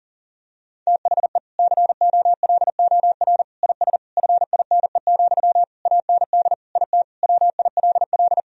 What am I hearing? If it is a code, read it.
Morse code, heard as a keyed tone, about 40 wpm, 700 Hz. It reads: THE COLOR IS FINE, AND IT WILL
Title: Sentences from Top 1000 Words - Review Part 2 - 40wpm